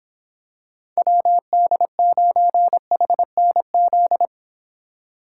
Morse code, heard as a keyed tone, about 26 wpm, 700 Hz. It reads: WD9HNZ